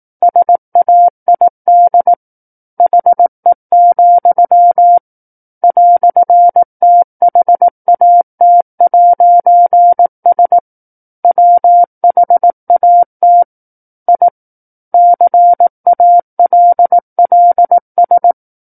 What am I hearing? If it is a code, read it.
Morse code, heard as a keyed tone, about 18 wpm, 700 Hz. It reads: SAID HE, "THAT'S WHAT I CALLS
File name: 18 WPM